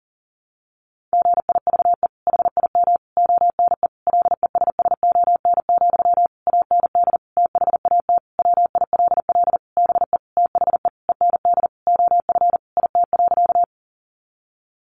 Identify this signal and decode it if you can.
Morse code, heard as a keyed tone, about 40 wpm, 700 Hz. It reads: GIVE HIM ONE LESSON, AND THAT WILL BE THE END OF IT.
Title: Sentences from Top 1200 Words 40wpm